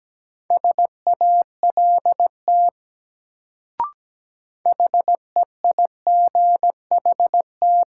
Morse code, heard as a keyed tone, about 17 wpm, 700 Hz. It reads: SALT HEIGHT
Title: Sets of 5 Words - New Word Drill - 17wpm